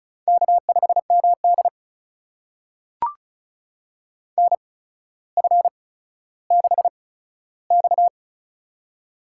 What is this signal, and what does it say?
Morse code, heard as a keyed tone, about 36 wpm, spaced as for 15 wpm, 700 Hz. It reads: K5MD NF6X
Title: USA Call Signs - 4 Characters - 8x Character Spacing - 35wpm